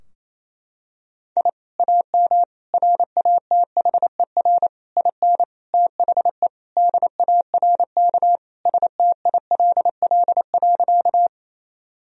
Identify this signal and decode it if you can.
Morse code, heard as a keyed tone, about 28 wpm, 700 Hz. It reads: I AM RATHER IN THE DARK STILL.
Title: Sentences from Top 1000 Words - Review Part 4 - 28wpm